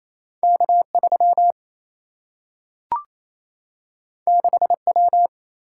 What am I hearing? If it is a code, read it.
Morse code, heard as a keyed tone, about 28 wpm, 700 Hz. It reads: K3 6W
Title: Single-Letter Single-Number 28wpm